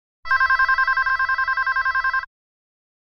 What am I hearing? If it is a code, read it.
There is no code.